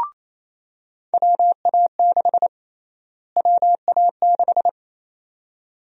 Morse code, heard as a keyed tone, about 28 wpm, 700 Hz. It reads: WA6 WA6
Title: USA Call Sign Prefixes - Repeated 2x - 28wpm